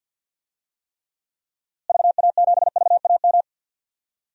Morse code, heard as a keyed tone, about 47 wpm, 700 Hz. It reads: VA7VAM
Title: DXCC-100 Call Signs - 6 Characters - Repeated 2x - 50wpm